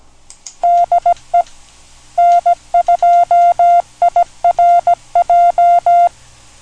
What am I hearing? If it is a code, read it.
Morse code, heard as a keyed tone, about 17 wpm, 700 Hz. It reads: DE N2IRJ
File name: TXVerticalRXBeverageSourceN2IRJ